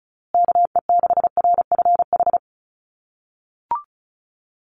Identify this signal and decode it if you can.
Morse code, heard as a keyed tone, about 34 wpm, 700 Hz. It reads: KE6RFH
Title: USA Call Signs Repeated 3x 35wpm